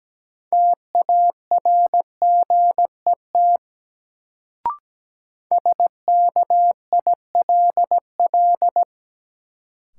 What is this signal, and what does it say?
Morse code, heard as a keyed tone, about 17 wpm, 700 Hz. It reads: TARGET SKILL
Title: Sentences from Top 1200 Words - New Word Drill Alt - 17wpm